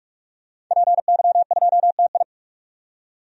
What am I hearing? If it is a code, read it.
Morse code, heard as a keyed tone, about 46 wpm, 700 Hz. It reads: PY2TI
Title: DXCC-100 Call Signs - 5 Characters - 45wpm